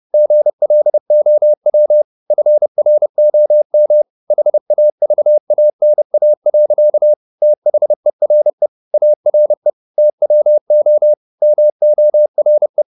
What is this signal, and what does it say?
Morse code, heard as a keyed tone, about 30 wpm, 600 Hz. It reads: GLOW FROM HAVANA. THERE ARE TWO MORE
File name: the old man and the sea / the old man and the sea 1 30wpm